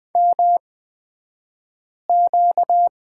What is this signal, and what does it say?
Morse code, heard as a keyed tone, about 20 wpm, spaced as for 5 wpm, 700 Hz. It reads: MQ
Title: 235 Lesson 38d - Letter q - Two Characters 20wpm